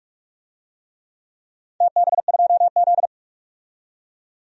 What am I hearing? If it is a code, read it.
Morse code, heard as a keyed tone, about 42 wpm, 700 Hz. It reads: TB2Z
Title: DXCC-100 Call Signs - 4 Characters - 45wpm